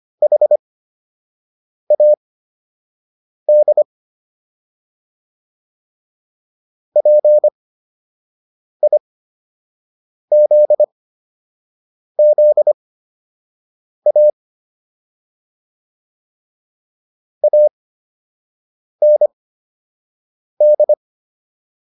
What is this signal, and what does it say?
Morse code, heard as a keyed tone, about 25 wpm, spaced as for 6 wpm, 600 Hz. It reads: HAD PIZZA AND